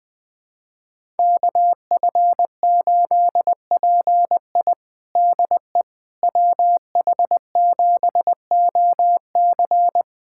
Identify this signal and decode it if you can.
Morse code, heard as a keyed tone, about 21 wpm, 700 Hz. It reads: KF8PI DE WH7OC
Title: USA Callsign DE Callsign - 5 Characters - 20wpm